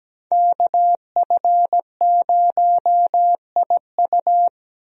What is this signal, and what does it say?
Morse code, heard as a keyed tone, about 17 wpm, 700 Hz. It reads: KF0IU
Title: USA Call Signs - 5 Characters - 17wpm